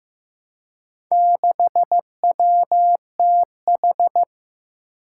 Morse code, heard as a keyed tone, about 15 wpm, 700 Hz. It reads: 6WTH
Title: Four Challenging Letter-Number Combinations 15wpm